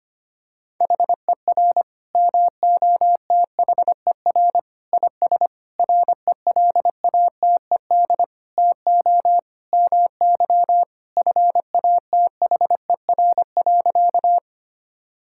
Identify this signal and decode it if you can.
Morse code, heard as a keyed tone, about 25 wpm, 700 Hz. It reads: HER MOTHER IS RELATED TO MY FATHER.
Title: Sentences from Top 1000 Words - Review Part 5 - 25wpm